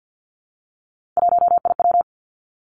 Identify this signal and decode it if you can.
Morse code, heard as a keyed tone, about 50 wpm, 700 Hz. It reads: 1IW
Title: Three Challenging Letter-Number Combinations 50wpm